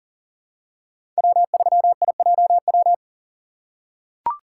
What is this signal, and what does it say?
Morse code, heard as a keyed tone, about 40 wpm, 700 Hz. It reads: W3IJW
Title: USA Call Signs - 5 Characters - 4x Character Spacing - 40wpm